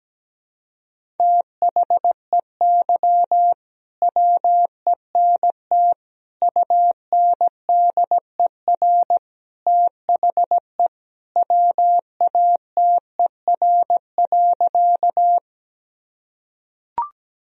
Morse code, heard as a keyed tone, about 17 wpm, 700 Hz. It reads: THEY WENT UNDER THE WATER.
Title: Sentences from Top 1000 Words - Review Part 1 - 17wpm